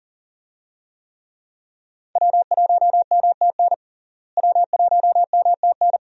Morse code, heard as a keyed tone, about 40 wpm, 700 Hz. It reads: W1MTN W1MTN
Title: USA Call Signs - 5 Characters - Repeated 3x - 40wpm